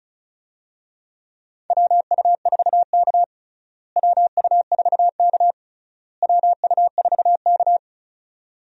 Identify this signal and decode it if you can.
Morse code, heard as a keyed tone, about 36 wpm, 700 Hz. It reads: WU4K WU4K WU4K
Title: USA Call Signs - 4 Characters - Repeated 3x - 35wpm